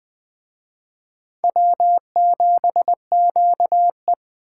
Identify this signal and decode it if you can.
Morse code, heard as a keyed tone, about 20 wpm, 700 Hz. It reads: W7QE